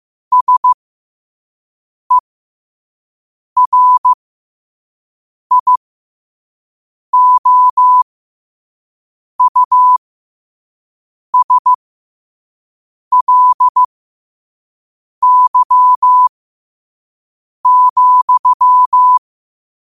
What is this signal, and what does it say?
Morse code, heard as a keyed tone, about 15 wpm, spaced as for 5 wpm, 1000 Hz. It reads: SERIOUSLY,
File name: qotdfiles / 5wpm